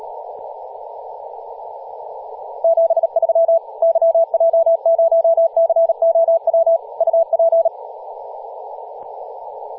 Morse code, heard as a keyed tone, about 38 wpm, 660 Hz, with radio noise. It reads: 73 YJ0COW UP
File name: posts / 2016-aug-26-2236z-yj0cow-18-073mhz-cw